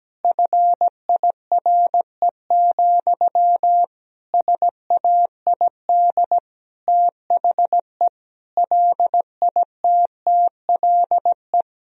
Morse code, heard as a keyed tone, about 17 wpm, 700 Hz. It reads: FIRE, SAID THE LITTLE
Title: Sentences from Top 1000 Words - Review Part 1 - 17wpm